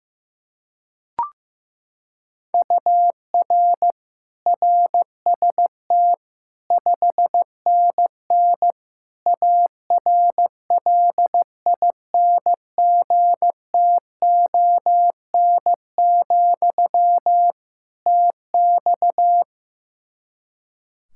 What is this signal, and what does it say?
Morse code, heard as a keyed tone, about 15 wpm, 700 Hz. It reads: UR RST 5NN ARLINGTON, TX
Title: Practice QSO - Exchange 1 - Beginning - 15wpm